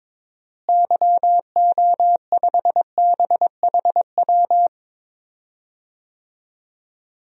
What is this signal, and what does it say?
Morse code, heard as a keyed tone, about 21 wpm, 700 Hz. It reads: YO5BHW
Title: DXCC-100 Call Signs - 6 Characters - Repeated 2x - 22wpm